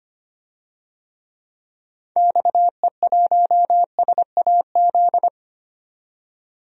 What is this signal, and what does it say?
Morse code, heard as a keyed tone, about 25 wpm, 700 Hz. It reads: XE1SAZ
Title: DXCC-100 Call Signs - 6 Characters - 25wpm